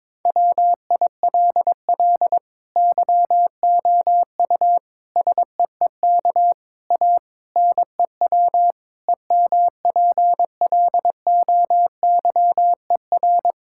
Morse code, heard as a keyed tone, about 22 wpm, 700 Hz. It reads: WILL YOU SEEK A NEW EMPLOYER
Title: Sentences from Top 1600 Words 22wpm